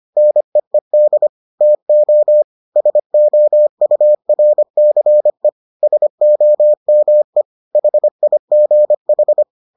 Morse code, heard as a keyed tone, about 25 wpm, 600 Hz. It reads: NEED TO SOURCE SOME HIGH